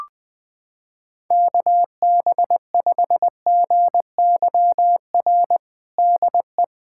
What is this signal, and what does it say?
Morse code, heard as a keyed tone, about 20 wpm, 700 Hz. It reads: KB5GYR DE
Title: USA Callsign DE Callsign 20wpm